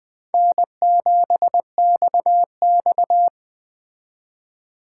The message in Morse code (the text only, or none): N7XX